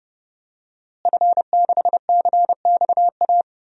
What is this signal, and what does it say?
Morse code, heard as a keyed tone, about 29 wpm, 700 Hz. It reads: F6CXA
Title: DXCC-100 Call Signs - 5 Characters - Repeated 3x - 30wpm